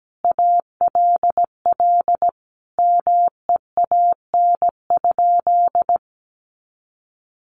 Morse code, heard as a keyed tone, about 17 wpm, 700 Hz. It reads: ALL MEAN?
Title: Sentences from Top 200 Words 17wpm